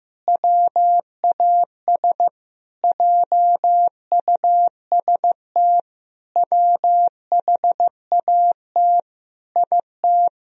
Morse code, heard as a keyed tone, about 15 wpm, 700 Hz. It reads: WAS JUST WHAT IT
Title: Sentences from Top 1000 Words - Review Part 4 - 15wpm